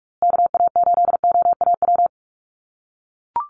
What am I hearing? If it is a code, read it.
Morse code, heard as a keyed tone, about 44 wpm, 700 Hz. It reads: KA8OAW
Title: USA Call Signs Repeated 3x 45wpm